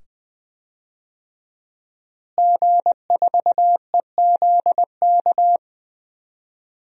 Morse code, heard as a keyed tone, about 20 wpm, 700 Hz. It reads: G4EZK
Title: DXCC-100 Call Signs - 5 Characters - Repeated 2x - 20wpm